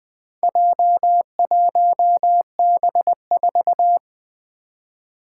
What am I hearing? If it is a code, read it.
Morse code, heard as a keyed tone, about 20 wpm, 700 Hz. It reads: J1B4
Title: Four Challenging Letter-Number Combinations 20wpm